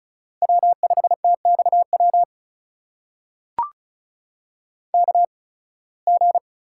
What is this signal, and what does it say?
Morse code, heard as a keyed tone, about 36 wpm, spaced as for 23 wpm, 700 Hz. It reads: W5TXW KG
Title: USA Call Signs - 5 Characters - 8x Character Spacing - 35wpm